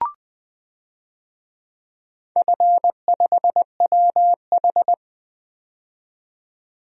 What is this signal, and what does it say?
Morse code, heard as a keyed tone, about 21 wpm, 700 Hz. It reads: F5WH
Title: DXCC-100 Call Signs - 4 Characters - 20wpm